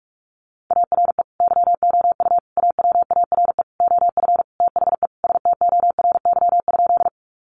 Morse code, heard as a keyed tone, about 44 wpm, 700 Hz. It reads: ARE YOU AWARE OF THE STORY?
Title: Sentences from Top 1200 Words 45wpm